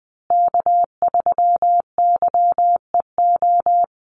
Morse code, heard as a keyed tone, about 19 wpm, 700 Hz. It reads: K3YEO
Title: Call Signs Repeated 2x - 20wpm